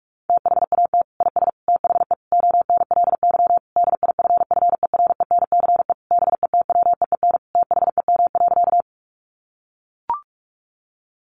Morse code, heard as a keyed tone, about 44 wpm, 700 Hz. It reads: THAT IS THE ONLY DIFFERENCE BETWEEN THEM.
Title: Sentences from Top 800 Words 45wpm